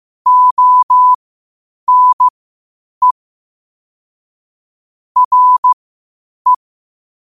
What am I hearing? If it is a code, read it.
Morse code, heard as a keyed tone, about 15 wpm, spaced as for 8 wpm, 1000 Hz. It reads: ONE RE
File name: qotdfiles / 8wpm